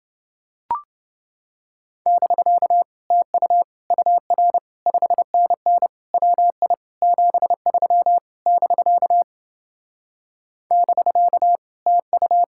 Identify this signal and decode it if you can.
Morse code, heard as a keyed tone, about 29 wpm, 700 Hz. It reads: <BK> TU UR 5NN WI 73 <BK> <BK> TU